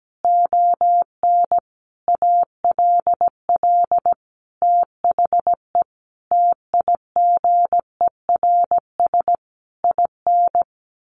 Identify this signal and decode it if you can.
Morse code, heard as a keyed tone, about 17 wpm, 700 Hz. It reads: ON ALL THE TIGERS IN
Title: A Sound Of Thunder 17wpm